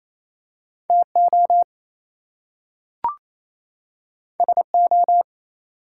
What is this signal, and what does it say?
Morse code, heard as a keyed tone, about 28 wpm, 700 Hz. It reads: TO SO
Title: Two Letter Words - 28wpm